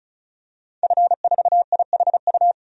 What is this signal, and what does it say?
Morse code, heard as a keyed tone, about 36 wpm, 700 Hz. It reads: F4IHU
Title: DXCC-100 Call Signs - 5 Characters - 35wpm